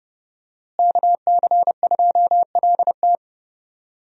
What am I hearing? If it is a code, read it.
Morse code, heard as a keyed tone, about 30 wpm, 700 Hz. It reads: KC2LT
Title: USA Call Signs - 5 Characters - Repeated 2x - 30wpm